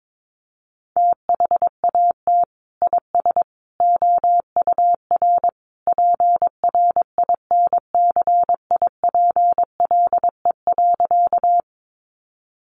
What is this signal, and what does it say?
Morse code, heard as a keyed tone, about 22 wpm, 700 Hz. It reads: THAT IS OUR PRINCIPLE.
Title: Sentences from Top 1400 Words 22wpm